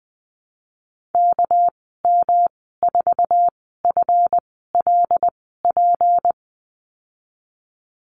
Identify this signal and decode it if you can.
Morse code, heard as a keyed tone, about 20 wpm, spaced as for 14 wpm, 700 Hz. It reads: KM4FLP
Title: 219 Lesson 35g - Number 0 - Call Signs 20wpm